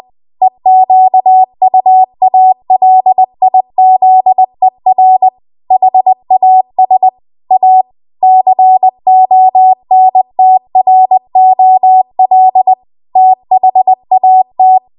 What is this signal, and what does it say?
Morse code, heard as a keyed tone, about 20 wpm, 750 Hz. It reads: EQUALIZER HAS A CONTROL THAT